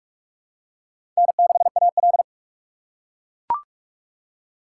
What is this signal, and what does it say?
Morse code, heard as a keyed tone, about 42 wpm, 700 Hz. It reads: N6AL
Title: USA Call Signs - 4 Characters - 3x Character Spacing - 45wpm